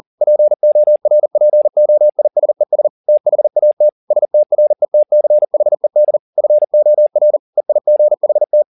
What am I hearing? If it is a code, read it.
Morse code, heard as a keyed tone, about 40 wpm, 600 Hz. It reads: PORPOISES THAT STRETCHED FOR EIGHT
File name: the old man and the sea 2 40wpm